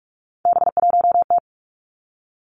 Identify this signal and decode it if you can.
Morse code, heard as a keyed tone, about 46 wpm, 700 Hz. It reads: B1T
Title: Three Challenging Letter-Number Combinations 45wpm